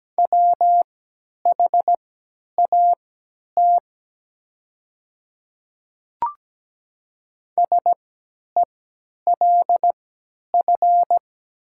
Morse code, heard as a keyed tone, about 17 wpm, spaced as for 10 wpm, 700 Hz. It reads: WHAT SELF
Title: 3x Character Spacing - Top 300 Words - 17wpm